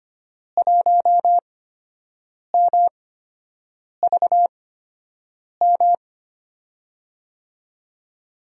Morse code, heard as a keyed tone, about 25 wpm, spaced as for 7 wpm, 700 Hz. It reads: 1MVM
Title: USA Call Signs - 8x Character Spacing - 25wpm